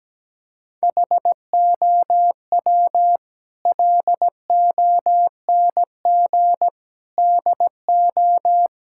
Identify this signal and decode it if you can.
Morse code, heard as a keyed tone, about 17 wpm, 700 Hz. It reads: HOW LONG DO
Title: Sentences from Top 1600 Words 17wpm